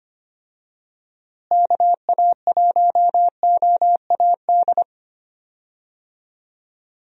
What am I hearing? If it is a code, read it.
Morse code, heard as a keyed tone, about 25 wpm, 700 Hz. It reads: KA1OAD
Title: USA Call Signs - 6 Characters - Repeated 3x - 25wpm